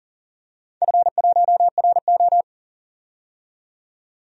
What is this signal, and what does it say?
Morse code, heard as a keyed tone, about 38 wpm, 700 Hz. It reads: F1RO
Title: DXCC-100 Call Signs - 4 Characters - 40wpm